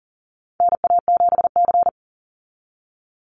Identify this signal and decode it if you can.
Morse code, heard as a keyed tone, about 40 wpm, 700 Hz. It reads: NA7C
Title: Call Signs Repeated 2x - 40wpm